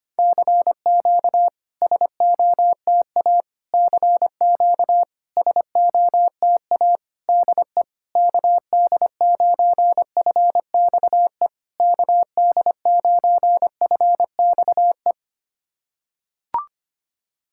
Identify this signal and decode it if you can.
Morse code, heard as a keyed tone, about 25 wpm, 700 Hz. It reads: CQ SOTA CQ SOTA DE KD9FXE KD9FXE
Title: SOTA QSO Practice 25wpm